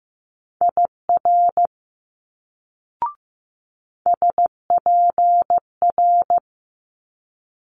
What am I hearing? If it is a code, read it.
Morse code, heard as a keyed tone, about 15 wpm, 700 Hz. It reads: IR SPR
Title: Multi-letter Phonemes Quick Fire 15wpm